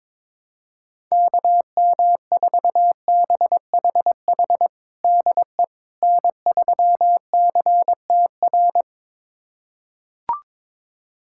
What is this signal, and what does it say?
Morse code, heard as a keyed tone, about 22 wpm, 700 Hz. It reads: KM4BHH DE N3CTR
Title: USA Callsign DE Callsign 22wpm